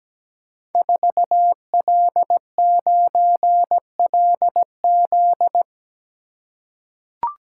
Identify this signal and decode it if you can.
Morse code, heard as a keyed tone, about 17 wpm, 700 Hz. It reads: 4L9LZ
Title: Code Groups - 5 Letter:Numbers - 17wpm